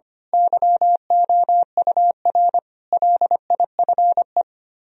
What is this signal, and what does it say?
Morse code, heard as a keyed tone, about 25 wpm, 700 Hz. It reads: YOUR LIFE
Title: Speed-Racing - Sets of 5 Words 25wpm